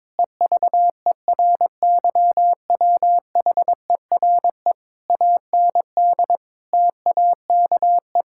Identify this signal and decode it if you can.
Morse code, heard as a keyed tone, about 22 wpm, 700 Hz. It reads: EVERYWHERE AND TAKE
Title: Sentences from Top 1600 Words 22wpm